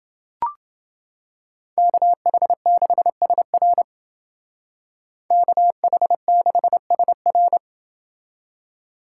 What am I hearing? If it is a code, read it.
Morse code, heard as a keyed tone, about 28 wpm, 700 Hz. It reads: KH6SR KH6SR